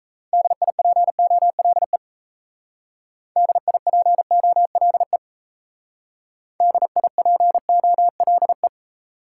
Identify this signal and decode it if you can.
Morse code, heard as a keyed tone, about 36 wpm, 700 Hz. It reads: DIPOLE DIPOLE DIPOLE